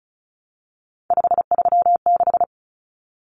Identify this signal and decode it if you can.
Morse code, heard as a keyed tone, about 34 wpm, 700 Hz. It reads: 536
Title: Numbers - 3-Digit 35wpm